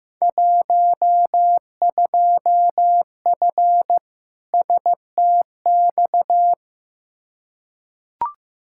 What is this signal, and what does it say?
Morse code, heard as a keyed tone, about 15 wpm, 700 Hz. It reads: 12F STX
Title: Field Day Exchanges Encore - 15wpm